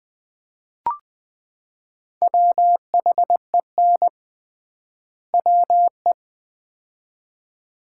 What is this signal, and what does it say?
Morse code, heard as a keyed tone, about 20 wpm, 700 Hz. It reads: WHEN WE
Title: 3x Word-Spacing - Sets of 2 Words - 20wpm